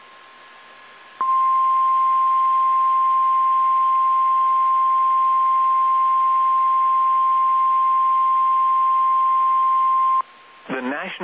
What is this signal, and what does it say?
Emergency Alert System (SAME) tones (the 1050 Hz attention tone), with speech. It is heard from 1.2 to 10.2 s.
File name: eas / KSDZKDJL / KSDZKDJL